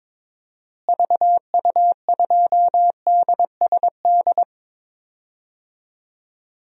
Morse code, heard as a keyed tone, about 22 wpm, 700 Hz. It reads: VU2DSD
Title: DXCC-100 Call Signs - 6 Characters - Repeated 2x - 22wpm